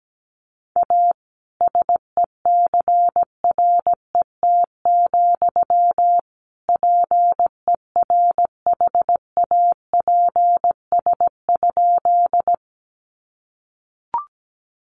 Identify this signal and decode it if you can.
Morse code, heard as a keyed tone, about 17 wpm, 700 Hz. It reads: A SECRET, PERHAPS?
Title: Sentences from Top 1200 Words 17wpm